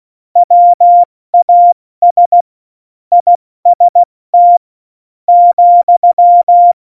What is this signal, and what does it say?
Morse code, heard as a keyed tone, about 16 wpm, 700 Hz. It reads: WAS IST ,
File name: Morsekurs / Witze / Witz 28. August 2018, 70 F80